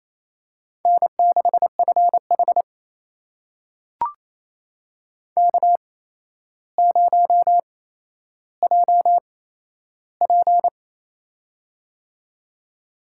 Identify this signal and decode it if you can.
Morse code, heard as a keyed tone, about 28 wpm, spaced as for 12 wpm, 700 Hz. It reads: N6FH K0JP
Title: USA Call Signs - 4 Characters - 8x Character Spacing - 28wpm